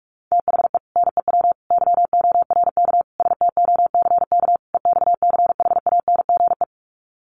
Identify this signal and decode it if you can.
Morse code, heard as a keyed tone, about 44 wpm, 700 Hz. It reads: THE NEW YORK STOCK EXCHANGE
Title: Sets of 5 Words 45wpm